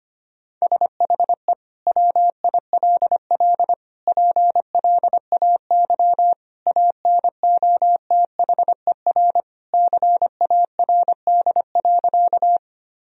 Morse code, heard as a keyed tone, about 25 wpm, 700 Hz. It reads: SHE WILL PLAY ANOTHER CARD.
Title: Sentences from Top 1000 Words - Review Part 4 - 25wpm